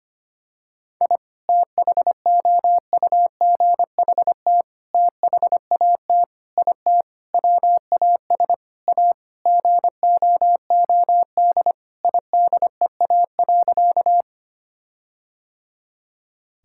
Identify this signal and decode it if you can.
Morse code, heard as a keyed tone, about 25 wpm, 700 Hz. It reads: I THOUGHT THAT IT WAS A GOOD IDEA.
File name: practice / 25wpm / Sentences with 4-Word Phrases - 25wpm